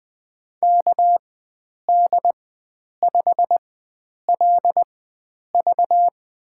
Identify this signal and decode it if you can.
Morse code, heard as a keyed tone, about 20 wpm, spaced as for 9 wpm, 700 Hz. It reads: KD5LV
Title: USA Call Signs - 5 Characters - 4x Character Spacing - 20wpm